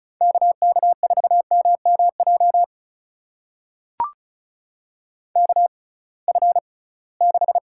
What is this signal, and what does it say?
Morse code, heard as a keyed tone, about 36 wpm, spaced as for 23 wpm, 700 Hz. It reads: KK4MMJ KF6
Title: USA Call Signs - 6 Characters - 6x Character Spacing - 35wpm